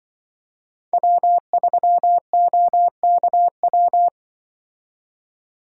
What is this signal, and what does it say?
Morse code, heard as a keyed tone, about 25 wpm, 700 Hz. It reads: W3OKW
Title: Speed Racing - USA Call Signs 20wpm